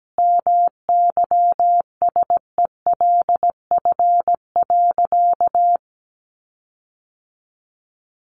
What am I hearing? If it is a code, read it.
Morse code, heard as a keyed tone, about 17 wpm, 700 Hz. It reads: MYSELF.
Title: Sentences from Top 1300 Words 17wpm